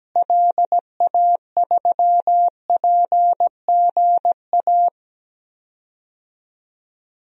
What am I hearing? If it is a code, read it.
Morse code, heard as a keyed tone, about 17 wpm, 700 Hz. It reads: LA3PGA
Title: DXCC-100 Call Signs - 6 Characters - Repeated 3x - 17wpm